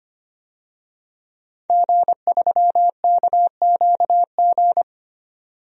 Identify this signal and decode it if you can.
Morse code, heard as a keyed tone, about 25 wpm, 700 Hz. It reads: G3KQG
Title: DXCC-100 Call Signs - 5 Characters - 25wpm